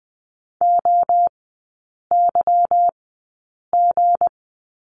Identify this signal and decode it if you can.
Morse code, heard as a keyed tone, about 20 wpm, spaced as for 8 wpm, 700 Hz. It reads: OYG
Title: 198 Lesson 32e - Letter k - Three Characters 20wpm